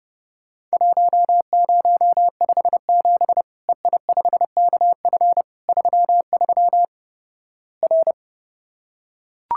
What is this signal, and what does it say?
Morse code, heard as a keyed tone, about 30 wpm, 700 Hz. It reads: 1057 EI5KF 33 R
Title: Worked All Europe DX Contest - QTC Exchange - 30wpm